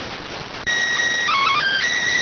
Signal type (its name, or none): SSTV (slow-scan television)